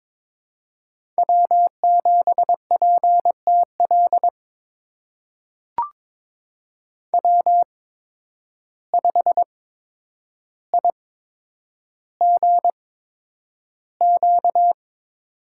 Morse code, heard as a keyed tone, about 22 wpm, spaced as for 9 wpm, 700 Hz. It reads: W7PTL W5IGQ
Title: USA Call Signs - 8x Character Spacing - 22wpm